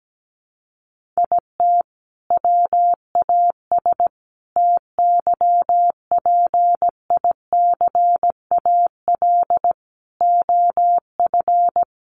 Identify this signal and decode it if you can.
Morse code, heard as a keyed tone, about 17 wpm, 700 Hz. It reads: IT WAS TYPICAL OF
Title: Sentences from Top 1100 Words 17wpm